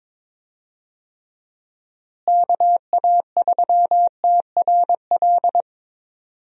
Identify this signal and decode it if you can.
Morse code, heard as a keyed tone, about 22 wpm, 700 Hz. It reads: KA3TRL
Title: USA Call Signs - 6 Characters - Repeated 3x - 22wpm